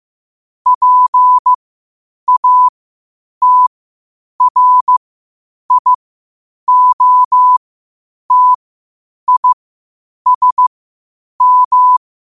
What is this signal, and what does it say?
Morse code, heard as a keyed tone, about 15 wpm, spaced as for 8 wpm, 1000 Hz. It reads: PATRIOTISM